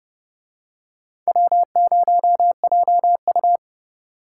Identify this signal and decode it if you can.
Morse code, heard as a keyed tone, about 30 wpm, 700 Hz. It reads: W0JU